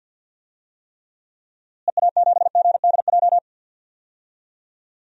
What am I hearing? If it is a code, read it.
Morse code, heard as a keyed tone, about 53 wpm, 700 Hz. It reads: EA7GDJ
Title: DXCC-100 Call Signs - 6 Characters - Repeated 3x - 50wpm